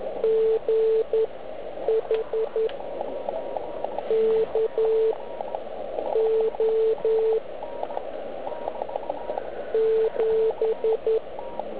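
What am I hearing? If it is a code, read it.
Morse code, heard as a keyed tone, about 11 wpm, spaced as for 8 wpm, 440 Hz. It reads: GH K O 7
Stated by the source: Slow (QRS) CW test by beginner OK1IFW during beginner Morse skeds on 3559.5 kHz (80 m).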